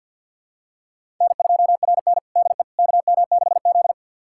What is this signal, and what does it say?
Morse code, heard as a keyed tone, about 51 wpm, 700 Hz. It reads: N2RN DE KM6Z
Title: USA Callsign DE Callsign - 4 Characters - 50wpm